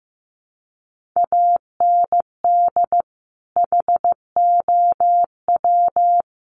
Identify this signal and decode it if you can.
Morse code, heard as a keyed tone, about 15 wpm, 700 Hz. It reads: AND HOW
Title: Sentences from Top 1200 Words 15wpm